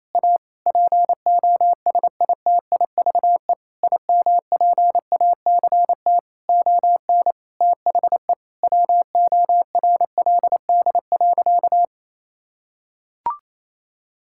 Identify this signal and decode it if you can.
Morse code, heard as a keyed tone, about 28 wpm, 700 Hz. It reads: A POSITIVE IMPACT ON THE WORLD.
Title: Sentences from Top 1000 Words - Review Part 6 - 28wpm